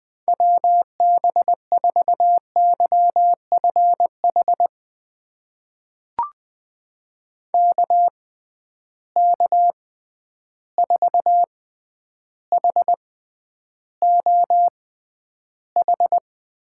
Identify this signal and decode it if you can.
Morse code, heard as a keyed tone, about 20 wpm, spaced as for 10 wpm, 700 Hz. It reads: WB4YFH KK4HOH
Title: USA Call Signs - 6 Characters - 6x Character Spacing - 20wpm